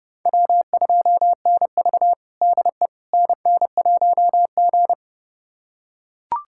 Morse code, heard as a keyed tone, about 30 wpm, 700 Hz. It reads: W2NV DE NN1G
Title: USA Callsign DE Callsign - 4 Characters - 30wpm